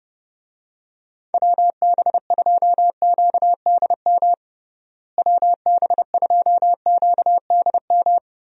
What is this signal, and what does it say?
Morse code, heard as a keyed tone, about 31 wpm, 700 Hz. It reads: WB2QDM WB2QDM
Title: USA Call Signs - 6 Characters - Repeated 3x - 30wpm